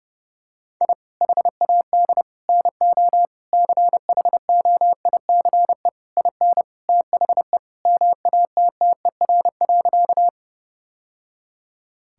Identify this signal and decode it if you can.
Morse code, heard as a keyed tone, about 30 wpm, 700 Hz. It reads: I HAD NO CHOICE IN THE MATTER.
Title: Sentences from Top 1000 Words - Review Part 6 - 30wpm